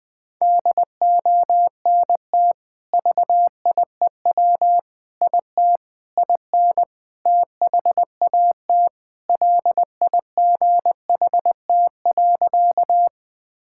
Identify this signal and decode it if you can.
Morse code, heard as a keyed tone, about 20 wpm, 700 Hz. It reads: DONT VIEW IT IN THAT LIGHT.
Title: Sentences from Top 1000 Words - Review Part 6 - 20wpm